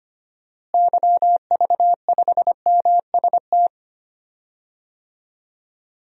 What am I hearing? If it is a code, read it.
Morse code, heard as a keyed tone, about 25 wpm, 700 Hz. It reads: YV5MST
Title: DXCC-100 Call Signs - 6 Characters - Repeated 2x - 25wpm